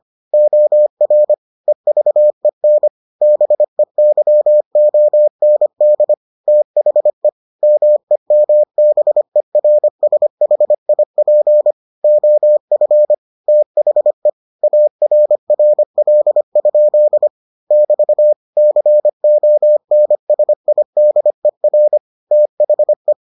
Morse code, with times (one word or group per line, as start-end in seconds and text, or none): 0.3-1.3 OR
1.7-2.9 EVEN
3.2-6.1 BEYOND
6.5-7.3 THE
7.6-11.7 MEMBERSHIP
12.0-13.1 OF
13.5-14.3 THE
14.6-17.3 ARRL?
17.7-18.3 <BT>
18.6-22.0 CONSIDER
22.3-23.1 THE